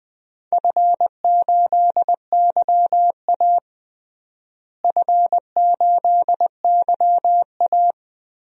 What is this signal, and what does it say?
Morse code, heard as a keyed tone, about 20 wpm, 700 Hz. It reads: F8YA F8YA
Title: DXCC-100 Call Signs - 4 Characters - Repeated 2x - 20wpm